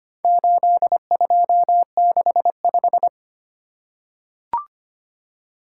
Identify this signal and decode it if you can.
Morse code, heard as a keyed tone, about 25 wpm, 700 Hz. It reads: 8265
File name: practice / 25wpm / Four Letter-Number Combinations - 25wpm